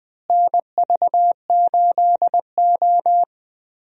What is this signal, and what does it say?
Morse code, heard as a keyed tone, about 20 wpm, 700 Hz. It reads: NV8O
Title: USA Call Signs - 4 Characters - Repeated 3x - 20wpm